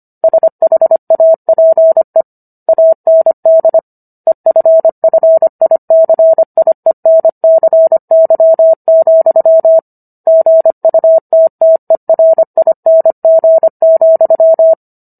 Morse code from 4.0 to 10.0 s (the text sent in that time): EFFICIENCY,